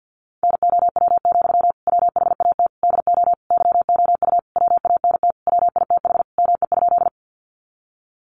Morse code, heard as a keyed tone, about 50 wpm, 700 Hz. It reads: NOW, WHAT DO YOU WANT WITH ME?
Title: Sentences from Top 200 Words 50wpm